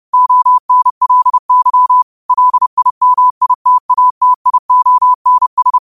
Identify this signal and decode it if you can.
Morse code, heard as a keyed tone, about 30 wpm, 1000 Hz. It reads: ONLY LIMITATIONS
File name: qotdfiles / 30wpm